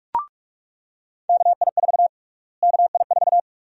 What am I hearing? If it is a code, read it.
Morse code, heard as a keyed tone, about 42 wpm, 700 Hz. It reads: KI4 KI4
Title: USA Call Sign Prefixes - Repeated 2x - 45wpm